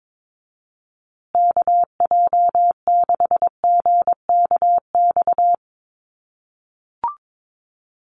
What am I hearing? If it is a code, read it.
Morse code, heard as a keyed tone, about 22 wpm, 700 Hz. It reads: KJ6GKX
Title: USA Call Signs Repeated 3x 22wpm